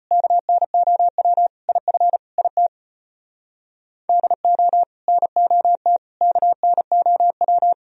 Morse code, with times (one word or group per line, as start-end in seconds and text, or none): 0.1-1.5 KNOW
1.7-2.2 IF
2.4-2.7 IT
4.1-4.8 DO
5.1-6.0 NOT
6.2-7.7 KNOW